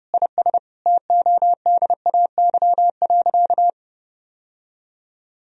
Morse code, heard as a keyed tone, about 32 wpm, 700 Hz. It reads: IS TODAY.
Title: Sentences from Top 1000 Words - Review Part 3 - 30wpm